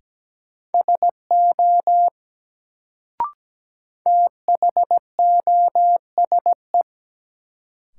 Morse code, heard as a keyed tone, about 17 wpm, 700 Hz. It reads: SO THOSE
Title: Sets of 2 Words - New Word Drill - 17wpm